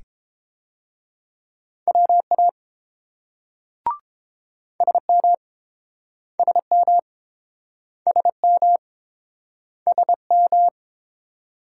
Morse code, heard as a keyed tone, about 30 wpm, 700 Hz. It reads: WA SM SM SM SM